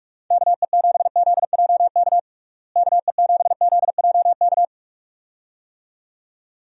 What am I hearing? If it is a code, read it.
Morse code, heard as a keyed tone, about 46 wpm, 700 Hz. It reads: KE7ZJK KE7ZJK
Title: USA Call Signs - 6 Characters - Repeated 2x - 45wpm